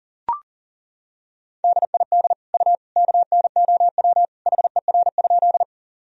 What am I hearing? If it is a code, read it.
Morse code, heard as a keyed tone, about 40 wpm, 700 Hz. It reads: DID U KNOW HER?
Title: Sentences from Top 100 Words and Abbreviations 40wpm